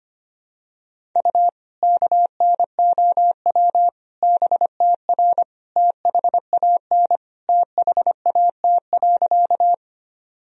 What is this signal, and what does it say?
Morse code, heard as a keyed tone, about 25 wpm, 700 Hz. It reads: U KNOW BTR THAN THAT.
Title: Sentences from Top 400 Words and Abbreviations 25wpm